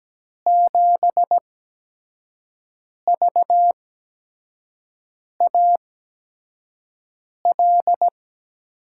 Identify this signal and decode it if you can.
Morse code, heard as a keyed tone, about 17 wpm, spaced as for 5 wpm, 700 Hz. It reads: 7VAL
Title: USA Call Signs - 6 Characters - 8x Character Spacing - 17wpm